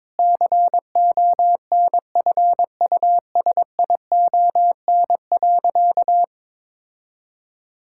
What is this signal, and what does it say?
Morse code, heard as a keyed tone, about 22 wpm, 700 Hz. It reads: CONFUSION.
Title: Sentences from Top 1600 Words 22wpm